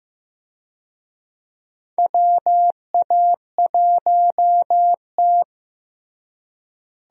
Morse code, heard as a keyed tone, about 16 wpm, 700 Hz. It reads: WA1T
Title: USA Call Signs - 4 Characters - Repeated 3x - 15wpm